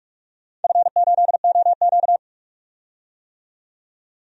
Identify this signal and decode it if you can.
Morse code, heard as a keyed tone, about 42 wpm, 700 Hz. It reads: F8OQ
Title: DXCC-100 Call Signs - 4 Characters - 45wpm